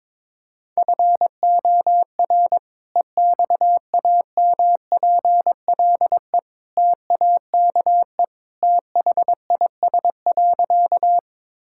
Morse code, heard as a keyed tone, about 22 wpm, 700 Hz. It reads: FOR EXAMPLE TAKE THIS.
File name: practice / 22wpm / Sentences with 2-Word Phrases - 22wpm